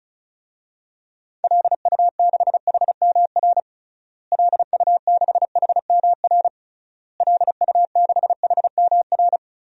Morse code, heard as a keyed tone, about 36 wpm, 700 Hz. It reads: LU6HMR LU6HMR LU6HMR
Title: DXCC-100 Call Signs - 6 Characters - Repeated 3x - 35wpm